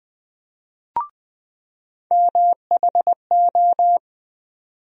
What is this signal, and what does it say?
Morse code, heard as a keyed tone, about 20 wpm, 700 Hz. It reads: MHO
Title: USA Call Sign Postfixes - 3 Letters 20wpm